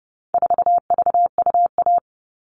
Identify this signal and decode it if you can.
Morse code, heard as a keyed tone, about 30 wpm, 700 Hz. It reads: 4VUA
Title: All Cut Numbers - 4-Digit 30wpm